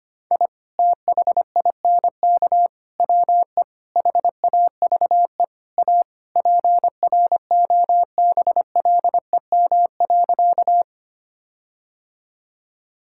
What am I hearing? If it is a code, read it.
Morse code, heard as a keyed tone, about 25 wpm, 700 Hz. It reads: I THINK WE HAVE A PROBLEM.
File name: practice / 25wpm / Sentences with 4-Word Phrases - 25wpm